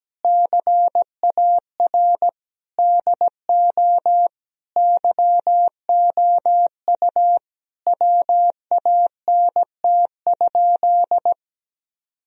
Morse code, heard as a keyed tone, about 17 wpm, 700 Hz. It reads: CAR DO YOU WANT?
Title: Sentences from Top 1000 Words - Review Part 1 - 17wpm